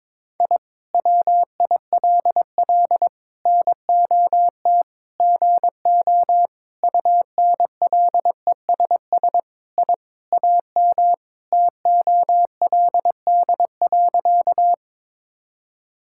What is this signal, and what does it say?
Morse code, heard as a keyed tone, about 22 wpm, 700 Hz. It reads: I WILL NOT GO UNLESS I AM TOLD.
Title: Sentences from Top 1000 Words - Review Part 6 - 22wpm